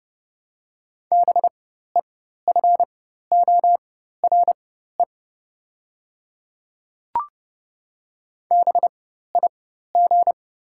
Morse code, heard as a keyed tone, about 30 wpm, spaced as for 14 wpm, 700 Hz. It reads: BEFORE BIG
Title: 4x Character Spacing - Top 300 Words - 30wpm